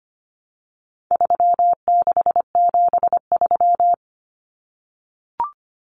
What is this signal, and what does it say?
Morse code, heard as a keyed tone, about 25 wpm, 700 Hz. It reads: 3673
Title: Common Cut Numbers - 4-Digit 25wpm